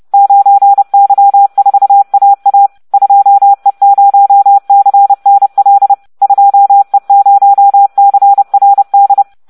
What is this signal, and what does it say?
Morse code, heard as a keyed tone, about 30 wpm, 800 Hz. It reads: 9Y4AA 2E0CNL 2E0CRD